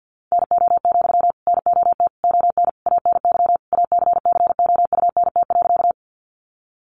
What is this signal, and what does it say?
Morse code, heard as a keyed tone, about 49 wpm, 700 Hz. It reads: NO, NOT ON ANY ACCOUNT.
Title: Sentences from Top 900 Words 50wpm